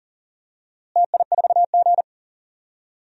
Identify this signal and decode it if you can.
Morse code, heard as a keyed tone, about 40 wpm, 700 Hz. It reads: TI4G
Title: DXCC-100 Call Signs - 4 Characters - Repeated 2x - 40wpm